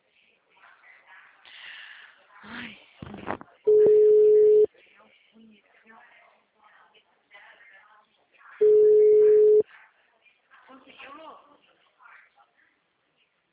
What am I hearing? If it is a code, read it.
Ringback tone (Europe, 425 Hz), with speech.
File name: out- -2010- - - .228